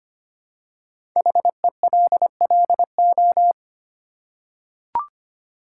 Morse code, heard as a keyed tone, about 25 wpm, 700 Hz. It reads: HELLO
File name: 5 Letter Words - First 3 Letters Given - 25wpm